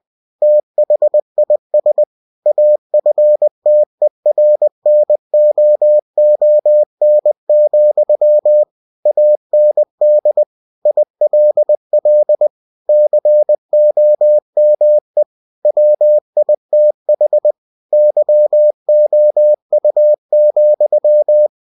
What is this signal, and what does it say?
Morse code, heard as a keyed tone, about 20 wpm, 600 Hz. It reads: THIS AFTERNOON, AND ILL COME WITH YOU,
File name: cw-books / winnie-the-pooh / winnie-20wpm-page-0024